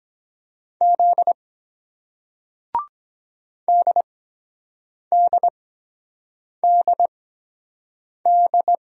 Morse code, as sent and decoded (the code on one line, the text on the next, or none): --.. / -.. -.. -.. -..
Z DDDD